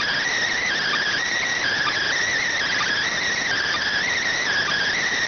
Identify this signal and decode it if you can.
Emergency Alert System (SAME) tones (header bursts), heard as harsh buzzing bursts. It is heard from 0.5 to 5.1 s.